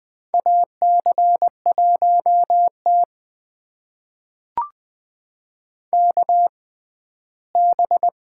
Morse code, heard as a keyed tone, about 21 wpm, spaced as for 14 wpm, 700 Hz. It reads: AC1T KB
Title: USA Call Signs - 6x Character Spacing - 20wpm